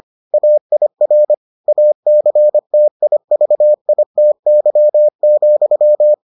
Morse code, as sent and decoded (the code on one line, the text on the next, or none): .- .. .-. / .- -.-. - .. ...- .. - -.-- --..--
AIR ACTIVITY,